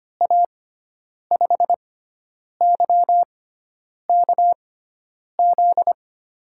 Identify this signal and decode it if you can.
Morse code, heard as a keyed tone, about 25 wpm, spaced as for 9 wpm, 700 Hz. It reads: A5YKZ
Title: USA Call Signs - 6x Character Spacing - 25wpm